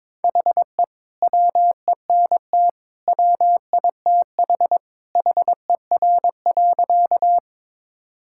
Morse code, heard as a keyed tone, about 22 wpm, 700 Hz. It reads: HE WENT WITH HER.
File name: practice / 22wpm / Sentences with 2-Word Phrases - 22wpm